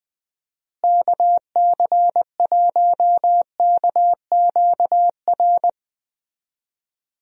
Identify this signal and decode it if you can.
Morse code, heard as a keyed tone, about 20 wpm, 700 Hz. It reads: KC1KQR
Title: USA Call Signs - 6 Characters - Repeated 2x - 20wpm